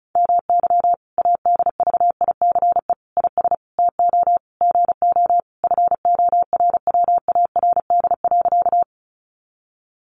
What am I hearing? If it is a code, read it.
Morse code, heard as a keyed tone, about 34 wpm, 700 Hz. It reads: MY ADVICE IS TO GO FORWARD.
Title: Sentences from Top 1300 Words 35wpm